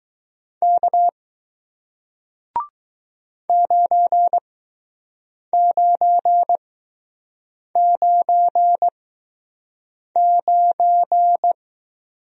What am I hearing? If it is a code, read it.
Morse code, heard as a keyed tone, about 20 wpm, spaced as for 6 wpm, 700 Hz. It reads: K 9999